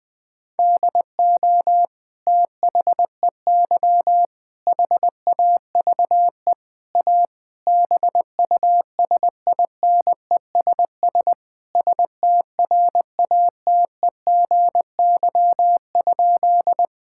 Morse code, heard as a keyed tone, about 20 wpm, 700 Hz. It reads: DO THEY HAVE A BUSINESS STRATEGY?
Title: Sentences from Top 1000 Words - Review Part 5 - 20wpm